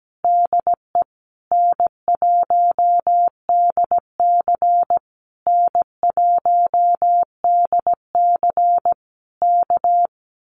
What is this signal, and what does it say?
Morse code, heard as a keyed tone, about 17 wpm, 700 Hz. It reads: DE N1DC N1DC K
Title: Common QSO Phrases Encore 17wpm